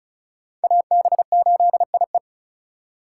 Morse code, heard as a keyed tone, about 46 wpm, 700 Hz. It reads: AB8IE